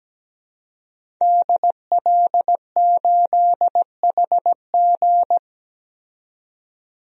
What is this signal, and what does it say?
Morse code, heard as a keyed tone, about 17 wpm, 700 Hz. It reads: DL8HG
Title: DXCC-100 Call Signs - 5 Characters - 17wpm